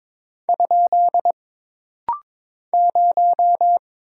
Morse code, heard as a keyed tone, about 22 wpm, 700 Hz. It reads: ? 0
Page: This rapid-fire timing is geared to push your ICR (Instant Character Recognition) skills.